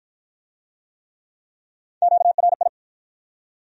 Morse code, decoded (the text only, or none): QRI